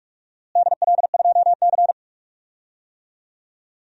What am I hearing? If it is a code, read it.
Morse code, heard as a keyed tone, about 47 wpm, 700 Hz. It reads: DL2C
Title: DXCC-100 Call Signs - 4 Characters - Repeated 2x - 45wpm